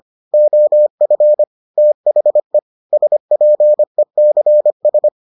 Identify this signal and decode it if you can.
Morse code, heard as a keyed tone, about 25 wpm, 600 Hz. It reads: OF THE SPECS